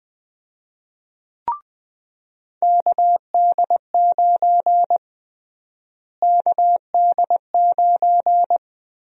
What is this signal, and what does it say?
Morse code, heard as a keyed tone, about 21 wpm, 700 Hz. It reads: KD9 KD9
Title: USA Call Sign Prefixes - Repeated 2x - 20wpm